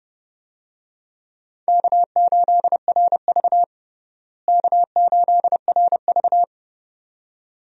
Morse code, heard as a keyed tone, about 30 wpm, 700 Hz. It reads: K8RV K8RV
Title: USA Call Signs - 4 Characters - Repeated 2x - 30wpm